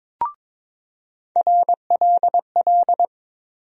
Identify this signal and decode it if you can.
Morse code, heard as a keyed tone, about 22 wpm, 700 Hz. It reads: RLL
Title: USA Call Sign Postfixes - 3 Letters 22wpm